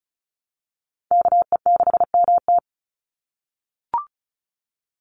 Morse code, heard as a keyed tone, about 34 wpm, 700 Hz. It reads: KE6MT
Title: NA SOTA Chaser Callsigns 35wpm